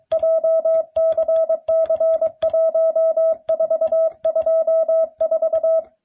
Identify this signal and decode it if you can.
Morse code, heard as a keyed tone, about 22 wpm, 660 Hz. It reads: JCC1424